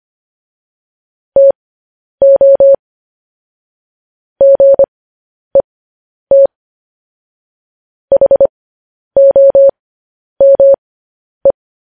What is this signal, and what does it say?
Morse code, heard as a keyed tone, about 25 wpm, spaced as for 10 wpm, 550 Hz. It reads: TO GET HOME